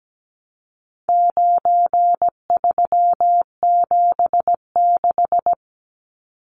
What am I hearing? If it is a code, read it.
Morse code, heard as a keyed tone, about 17 wpm, 700 Hz. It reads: 9376
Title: Numbers - 4-Digit 17wpm